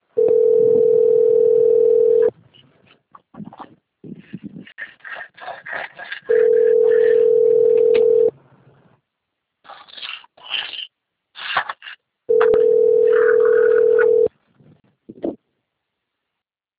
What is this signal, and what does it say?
Ringback tone (North America), 440 + 480 Hz.